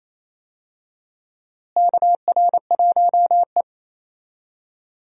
Morse code, heard as a keyed tone, about 28 wpm, 700 Hz. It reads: KR1E
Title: USA Call Signs - 4 Characters - Repeated 2x - 28wpm